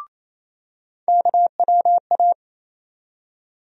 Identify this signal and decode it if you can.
Morse code, heard as a keyed tone, about 28 wpm, 700 Hz. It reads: KWA
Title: USA Call Sign Postfixes - 3 Letters 28wpm